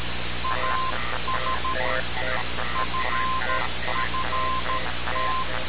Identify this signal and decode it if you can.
Morse code, heard as a keyed tone, about 13 wpm, 1040 Hz, with noise. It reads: AS FFT
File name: radiocap / REC z